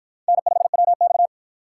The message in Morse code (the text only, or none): N5WX